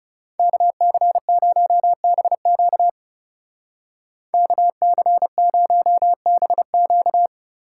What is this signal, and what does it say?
Morse code, heard as a keyed tone, about 37 wpm, 700 Hz. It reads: KC0BQ KC0BQ
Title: Speed Racing - USA 2x2 Call Signs 30wpm